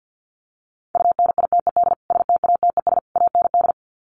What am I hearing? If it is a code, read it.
Morse code, heard as a keyed tone, about 50 wpm, 700 Hz. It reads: UNITED STATES AND
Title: Sets of 3 Words 50wpm